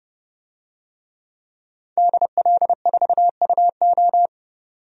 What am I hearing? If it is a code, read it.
Morse code, heard as a keyed tone, about 30 wpm, 700 Hz. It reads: DL4UO
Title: DXCC-100 Call Signs - 5 Characters - 30wpm